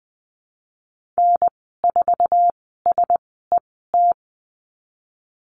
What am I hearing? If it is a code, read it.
Morse code, heard as a keyed tone, about 20 wpm, spaced as for 15 wpm, 700 Hz. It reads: N4SET
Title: 213 Lesson 34g - Number 8 - Call Signs 20wpm